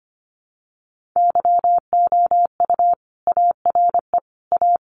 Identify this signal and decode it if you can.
Morse code, heard as a keyed tone, about 25 wpm, 700 Hz. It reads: YOU ARE A
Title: Sentences from Top 1500 Words 25wpm